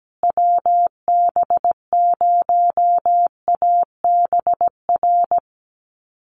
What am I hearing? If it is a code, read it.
Morse code, heard as a keyed tone, about 17 wpm, 700 Hz. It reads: WB0ABR
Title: USA Call Signs Repeated 3x 17wpm